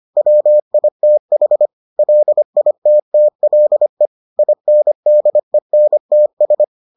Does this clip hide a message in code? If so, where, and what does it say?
Yes: Morse code from 0.2 to 6.6 s, heard as a keyed tone, about 25 wpm, 600 Hz. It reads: WITH LITTLE INDENTS